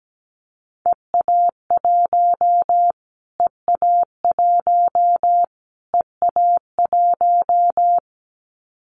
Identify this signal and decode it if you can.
Morse code, heard as a keyed tone, about 17 wpm, 700 Hz. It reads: EA1 EA1 EA1
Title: Callsign Prefixes for Top 25 DXCC - Repeated 3x - 17wpm